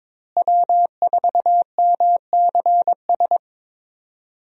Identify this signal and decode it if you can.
Morse code, heard as a keyed tone, about 22 wpm, 700 Hz. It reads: W4MCS
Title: USA Call Signs - 5 Characters - Repeated 3x - 22wpm